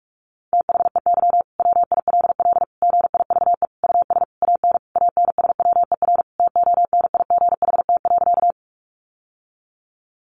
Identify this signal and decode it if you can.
Morse code, heard as a keyed tone, about 44 wpm, 700 Hz. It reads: THEY WILL GIVE US AN ANSWER TONIGHT.
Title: Sentences from Top 600 Words 45wpm